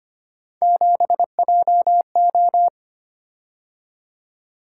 Morse code, heard as a keyed tone, about 25 wpm, 700 Hz. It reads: 7JO
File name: Three Letter-Number Combinations - 25wpm